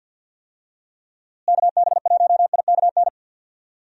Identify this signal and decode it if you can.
Morse code, heard as a keyed tone, about 49 wpm, 700 Hz. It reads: KB1IKN
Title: USA Call Signs - 6 Characters - Repeated 2x - 50wpm